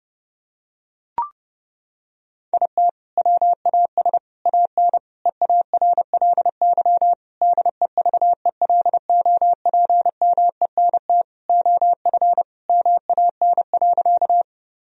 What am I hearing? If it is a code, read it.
Morse code, heard as a keyed tone, about 31 wpm, 700 Hz. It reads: IT WAS AN EARLY DEVELOPMENT OF MAN.
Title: Sentences from Top 1000 Words - Review Part 4 - 30wpm